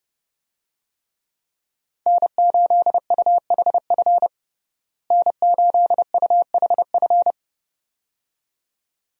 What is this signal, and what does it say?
Morse code, heard as a keyed tone, about 30 wpm, 700 Hz. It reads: N8UHF N8UHF
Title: USA Call Signs - 5 Characters - Repeated 2x - 30wpm